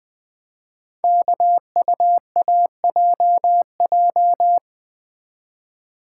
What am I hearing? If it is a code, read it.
Morse code, heard as a keyed tone, about 20 wpm, 700 Hz. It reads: KUAJJ
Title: Code Groups - 5 Letters - 20wpm